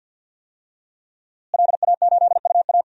Morse code, heard as a keyed tone, about 48 wpm, 700 Hz. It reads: LA8UA
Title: DXCC-100 Call Signs - 5 Characters - Repeated 3x - 50wpm